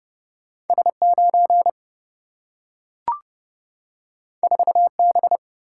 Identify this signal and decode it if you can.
Morse code, heard as a keyed tone, about 30 wpm, 700 Hz. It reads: S9 4B
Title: Single-Letter Single-Number 30wpm